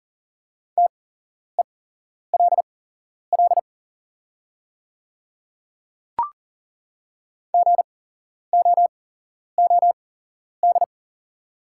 Morse code, heard as a keyed tone, about 38 wpm, spaced as for 11 wpm, 700 Hz. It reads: TELL GOOD
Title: 8x Character Spacing - Top 200 Words - 40wpm